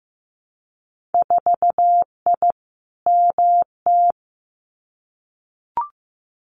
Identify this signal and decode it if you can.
Morse code, heard as a keyed tone, about 15 wpm, 700 Hz. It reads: 4I MT
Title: Winter Field Day Exchanges Encore 15wpm